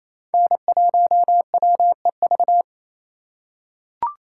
Morse code, heard as a keyed tone, about 28 wpm, 700 Hz. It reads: N1WEV
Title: USA Call Signs - 5 Characters - 2x Character Spacing - 28wpm